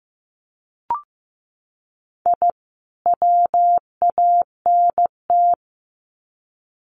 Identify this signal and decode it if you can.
Morse code, heard as a keyed tone, about 15 wpm, 700 Hz. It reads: I WANT
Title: Sets of 2 Words 15wpm